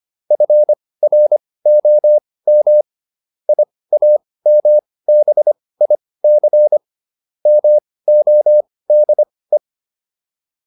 Morse code, heard as a keyed tone, about 25 wpm, spaced as for 18 wpm, 600 Hz. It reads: FROM IAMBIC MODE